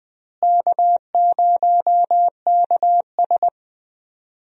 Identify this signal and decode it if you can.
Morse code, heard as a keyed tone, about 20 wpm, 700 Hz. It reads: K0KS
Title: USA Call Signs - 4 Characters - Repeated 2x - 20wpm